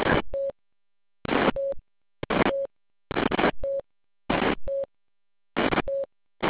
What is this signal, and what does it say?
There is no code.